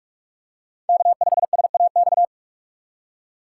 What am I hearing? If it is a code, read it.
Morse code, heard as a keyed tone, about 46 wpm, 700 Hz. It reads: K5SAX
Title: USA Call Signs - 5 Characters - 3x Character Spacing - 45wpm